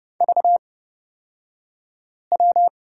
Morse code, heard as a keyed tone, about 30 wpm, spaced as for 5 wpm, 700 Hz. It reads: VW
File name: Lesson 36d - BT - Two Characters 30wpm